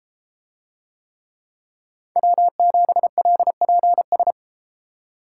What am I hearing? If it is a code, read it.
Morse code, heard as a keyed tone, about 32 wpm, 700 Hz. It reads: W7LPS